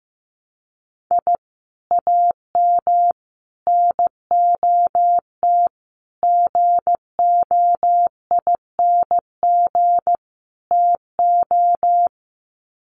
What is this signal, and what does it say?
Morse code, heard as a keyed tone, about 15 wpm, 700 Hz. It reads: I AM NOT GOING TO
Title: Sets of 5 Words Encore 15wpm